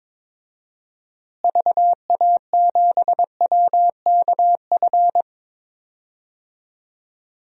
Morse code, heard as a keyed tone, about 22 wpm, 700 Hz. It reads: VA7WKF
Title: DXCC-100 Call Signs - 6 Characters - Repeated 3x - 22wpm